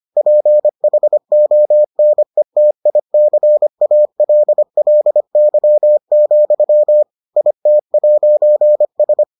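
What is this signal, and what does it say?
Morse code, heard as a keyed tone, about 25 wpm, 600 Hz. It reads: PHONETICALLY, IT'S
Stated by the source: Morse edition of the Foundations of Amateur Radio episodes.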